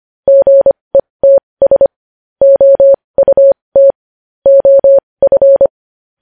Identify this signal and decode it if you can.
Morse code, heard as a keyed tone, about 25 wpm, spaced as for 20 wpm, 550 Hz. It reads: GETS OUT OF
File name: podcast / News headlines-1x-25x20